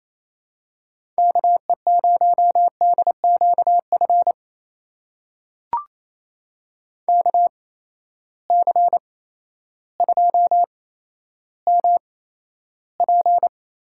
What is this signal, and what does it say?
Morse code, heard as a keyed tone, about 28 wpm, spaced as for 13 wpm, 700 Hz. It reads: KE0DQF KC2MP
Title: USA Call Signs - 6 Characters - 8x Character Spacing - 28wpm